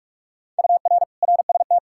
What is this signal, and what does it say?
Morse code, heard as a keyed tone, about 47 wpm, 700 Hz. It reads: UR RST
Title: Practice QSO - Exchange 1 - Beginning - 45wpm